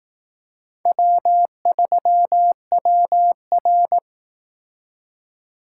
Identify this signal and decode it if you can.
Morse code, heard as a keyed tone, about 19 wpm, 700 Hz. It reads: W3WR